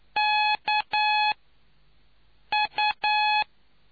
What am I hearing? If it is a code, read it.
Morse code, heard as a keyed tone, about 9 wpm, spaced as for 5 wpm, 850 Hz. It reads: KU